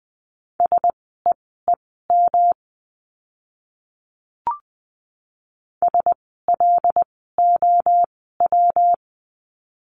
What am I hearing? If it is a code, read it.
Morse code, heard as a keyed tone, about 20 wpm, spaced as for 14 wpm, 700 Hz. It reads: SEEM SLOW
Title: 115 Lesson 19f - Letter w - Words 20wpm